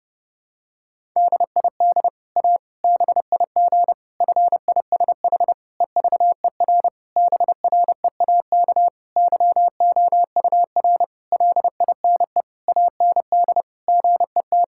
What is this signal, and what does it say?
Morse code, heard as a keyed tone, about 30 wpm, 700 Hz. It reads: DID A BIG FISH EVER BREAK YOUR LINE AND GET
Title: Sentences from Top 1000 Words - Review Part 6 - 30wpm